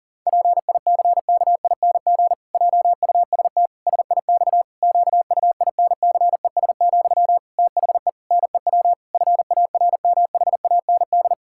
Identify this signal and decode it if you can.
Morse code, heard as a keyed tone, about 40 wpm, 700 Hz. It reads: PICKING JUST SIX QUINCES, THE NEW FARMHAND
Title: Pangram Challenge v2 - 40wpm